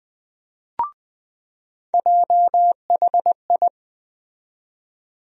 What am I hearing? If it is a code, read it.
Morse code, heard as a keyed tone, about 20 wpm, 700 Hz. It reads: JHI
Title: USA Call Sign Postfixes - 3 Letters 20wpm